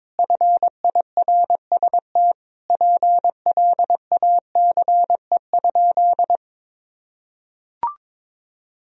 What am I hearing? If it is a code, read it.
Morse code, heard as a keyed tone, about 21 wpm, 700 Hz. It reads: FIRST PLACE?
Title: Sentences from Top 200 Words and Abbreviations 22wpm